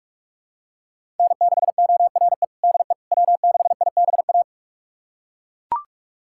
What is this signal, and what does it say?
Morse code, heard as a keyed tone, about 41 wpm, 700 Hz. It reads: N6ORE DE W6IBA
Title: USA Callsign DE Callsign - 5 Characters - 45wpm